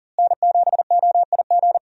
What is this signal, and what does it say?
Morse code, heard as a keyed tone, about 40 wpm, 700 Hz. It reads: N7OIG
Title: USA Call Signs - 5 Characters - Repeated 3x - 40wpm